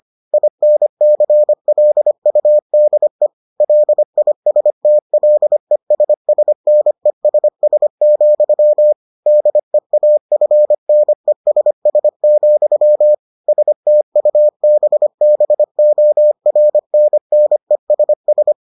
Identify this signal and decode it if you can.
Morse code, heard as a keyed tone, about 25 wpm, 600 Hz. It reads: INCLUDE LISTLESSNESS, DEAFNESS, STUBBORNNESS